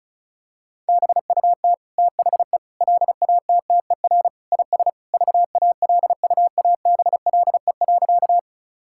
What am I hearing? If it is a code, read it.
Morse code, heard as a keyed tone, about 36 wpm, 700 Hz. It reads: BUT THE LATTER IS VALUABLE.
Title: Sentences from Top 1600 Words 35wpm